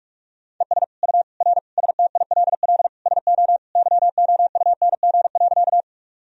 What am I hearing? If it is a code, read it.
Morse code, heard as a keyed tone, about 44 wpm, 700 Hz. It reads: ES U R STILL SO YOUNG.
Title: Sentences from Top 400 Words and Abbreviations 45wpm